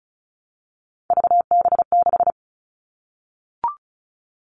Morse code, heard as a keyed tone, about 36 wpm, 700 Hz. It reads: VB6
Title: Three Challenging Letter-Number Combinations 35wpm